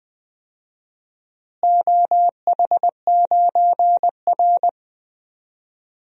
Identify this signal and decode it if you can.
Morse code, heard as a keyed tone, about 20 wpm, 700 Hz. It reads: OH9R
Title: DXCC-100 Call Signs - 4 Characters - Repeated 3x - 20wpm